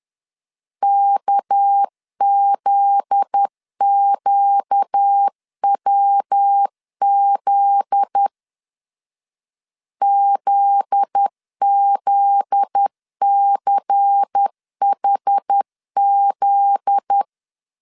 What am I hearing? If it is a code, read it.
Morse code, heard as a keyed tone, about 10 wpm, 800 Hz. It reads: KZQWZ ZZCHZ
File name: LEZIONI / 7 GF9Z / 7 GF9Z 10wpm 2611